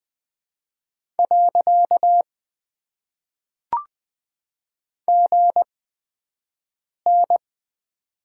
Morse code, heard as a keyed tone, about 20 wpm, spaced as for 5 wpm, 700 Hz. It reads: . GN
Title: Top 90 QSO Elements - 8x Character Spacing - 20wpm